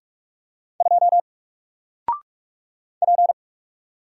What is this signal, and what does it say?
Morse code, heard as a keyed tone, about 44 wpm, 700 Hz. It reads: 2 P
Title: Single Character (Letter:Number:Punctuation:Prosign) Alt 45wpm